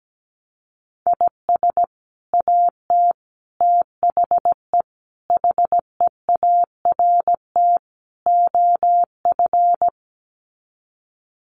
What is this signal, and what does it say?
Morse code, heard as a keyed tone, about 17 wpm, 700 Hz. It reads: IS AT THE HEART OF
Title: Sets of 5 Words Encore2 17wpm